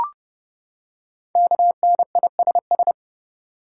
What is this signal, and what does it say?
Morse code, heard as a keyed tone, about 29 wpm, 700 Hz. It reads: KNISS
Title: Code Groups - 5 Letters - 30wpm